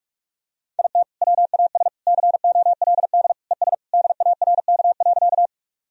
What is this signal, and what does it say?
Morse code, heard as a keyed tone, about 46 wpm, 700 Hz. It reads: IT WAS COLD ES DARK.
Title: Sentences from Top 500 Words and Abbreviations 45wpm